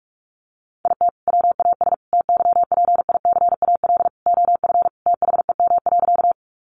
Morse code, heard as a keyed tone, about 44 wpm, 700 Hz. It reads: IT WAS TYPICAL OF THEM.
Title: Sentences from Top 1100 Words 45wpm